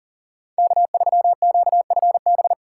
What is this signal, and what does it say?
Morse code, heard as a keyed tone, about 40 wpm, 700 Hz. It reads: K3QFB
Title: USA Call Signs - 5 Characters - 3x Character Spacing - 40wpm